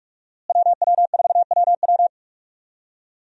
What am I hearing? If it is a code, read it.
Morse code, heard as a keyed tone, about 46 wpm, 700 Hz. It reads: WW4WW
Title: USA Call Signs - 5 Characters - Repeated 2x - 45wpm